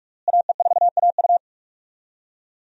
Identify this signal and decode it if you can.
Morse code, heard as a keyed tone, about 46 wpm, 700 Hz. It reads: AE4AU
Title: USA Call Signs - 5 Characters - Repeated 2x - 45wpm